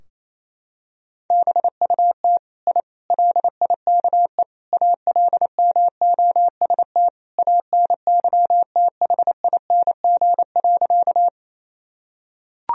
Morse code, heard as a keyed tone, about 28 wpm, 700 Hz. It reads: BUT I LIKE ALMOST ANYTHING.
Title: Sentences from Top 1000 Words - Review Part 4 - 28wpm